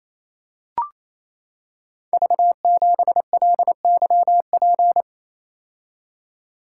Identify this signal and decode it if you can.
Morse code, heard as a keyed tone, about 27 wpm, 700 Hz. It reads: V7LYP
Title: Code Groups - 5 Letter:Numbers - 28wpm